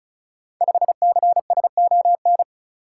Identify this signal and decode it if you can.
Morse code, heard as a keyed tone, about 36 wpm, 700 Hz. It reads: 5CSON
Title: Code Groups - 5 Letter:Numbers - 35wpm